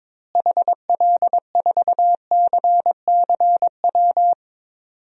Morse code, heard as a keyed tone, about 22 wpm, 700 Hz. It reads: HL4CCW
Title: DXCC-100 Call Signs - 6 Characters - Repeated 3x - 22wpm